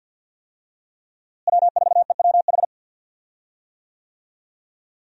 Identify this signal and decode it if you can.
Morse code, heard as a keyed tone, about 49 wpm, 700 Hz. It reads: W4EWH
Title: USA Call Signs - 5 Characters - Repeated 3x - 50wpm